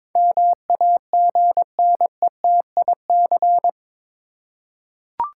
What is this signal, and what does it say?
Morse code, heard as a keyed tone, about 21 wpm, 700 Hz. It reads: MAGNETIC
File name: practice / 22wpm / Eight Letter Words - 22wpm